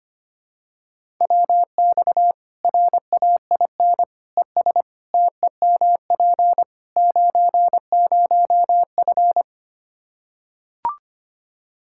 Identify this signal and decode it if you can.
Morse code, heard as a keyed tone, about 25 wpm, 700 Hz. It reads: WX RAIN ES TEMP 90F
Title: Practice QSO - Exchange 2 - Info - 25wpm